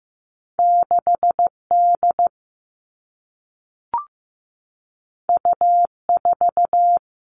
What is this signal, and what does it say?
Morse code, heard as a keyed tone, about 15 wpm, 700 Hz. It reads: 6D U4
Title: All Cut Numbers - 2-Digit 15wpm